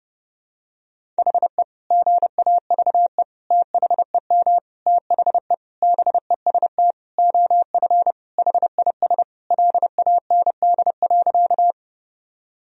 Morse code, heard as a keyed tone, about 30 wpm, 700 Hz. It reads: HE GAVE THEM THE BEST OF HIS LAND.
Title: Sentences from Top 1000 Words - Review Part 1 - 30wpm